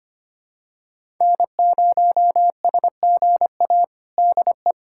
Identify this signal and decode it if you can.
Morse code, heard as a keyed tone, about 25 wpm, 700 Hz. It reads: N0SGA DE
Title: USA Callsign DE Callsign - 5 Characters - 25wpm